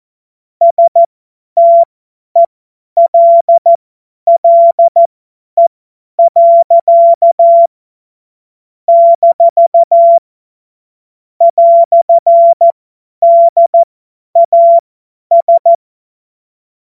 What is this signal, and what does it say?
Morse code, heard as a keyed tone, about 14 wpm, spaced as for 10 wpm, 680 Hz. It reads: STELLE. - "DAS
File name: Morsekurs / Witze / Witz 13. Dezember 2022 50 F70